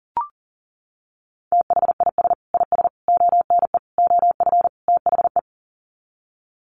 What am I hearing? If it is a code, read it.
Morse code, heard as a keyed tone, about 40 wpm, 700 Hz. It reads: THIS IS ONE OF THE
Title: Sets of 5 Words 40wpm